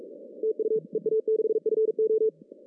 Morse code, heard as a keyed tone, about 41 wpm, 440 Hz. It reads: TU EA6FO
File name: CW / cqww-cw-k5tr